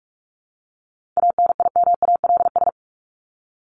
Morse code, heard as a keyed tone, about 44 wpm, 700 Hz. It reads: ANIMALS
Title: ES:S Suffix - Top 100 Words with Base Words 45wpm